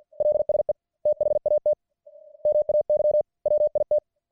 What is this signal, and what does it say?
Morse code, heard as a keyed tone, about 50 wpm, 600 Hz. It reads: LSE THAT MAY WIT